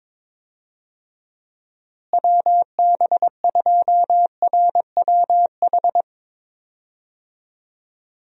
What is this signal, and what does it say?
Morse code, heard as a keyed tone, about 22 wpm, 700 Hz. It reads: WB2RWH
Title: USA Call Signs - 6 Characters - Repeated 3x - 22wpm